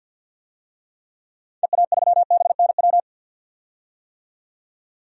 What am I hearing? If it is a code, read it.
Morse code, heard as a keyed tone, about 50 wpm, 700 Hz. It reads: EA3BNW
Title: DXCC-100 Call Signs - 6 Characters - 50wpm